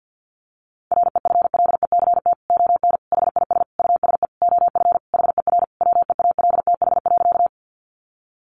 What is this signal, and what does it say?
Morse code, heard as a keyed tone, about 50 wpm, 700 Hz. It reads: REFLECT ON HIS USE OF HER WEALTH.
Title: Sentences from Top 1500 Words 50wpm